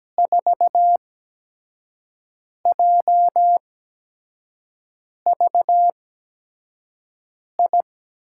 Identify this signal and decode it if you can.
Morse code, heard as a keyed tone, about 17 wpm, spaced as for 5 wpm, 700 Hz. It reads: 4JVI
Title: USA Call Signs - 6 Characters - 8x Character Spacing - 17wpm